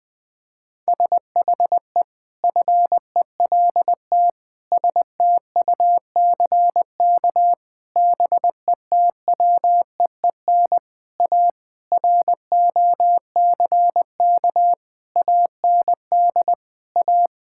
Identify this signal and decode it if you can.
Morse code, heard as a keyed tone, about 21 wpm, 700 Hz. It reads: SHE FELT STUCK BETWEEN A ROCK AND A
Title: Sentences with 8-Word Phrases - 20wpm